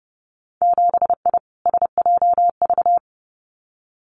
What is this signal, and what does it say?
Morse code, heard as a keyed tone, about 31 wpm, 700 Hz. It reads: 7I SJV
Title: Winter Field Day Exchanges Encore 30wpm